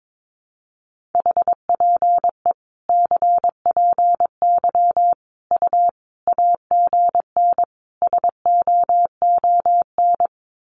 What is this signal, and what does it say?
Morse code, heard as a keyed tone, about 22 wpm, 700 Hz. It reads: HPE CPY U AGN SOON
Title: Common QSO Phrases 22wpm